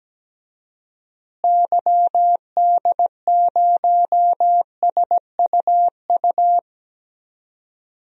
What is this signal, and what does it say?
Morse code, heard as a keyed tone, about 17 wpm, 700 Hz. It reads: YD0SUU
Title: DXCC-100 Call Signs - 6 Characters - Repeated 2x - 17wpm